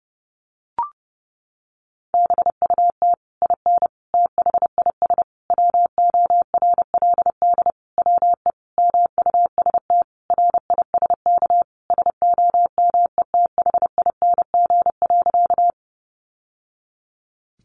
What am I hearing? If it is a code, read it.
Morse code, heard as a keyed tone, about 28 wpm, 700 Hz. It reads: BUT IN THIS WORLD WE MUST RISK SOMETHING.
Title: Sentences from Top 600 Words 30wpm